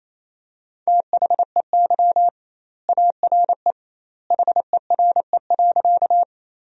Morse code, heard as a keyed tone, about 28 wpm, 700 Hz. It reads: THEY ARE HERE.
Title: 2x Word-Spacing - Sentences from Top 200 Words - 28wpm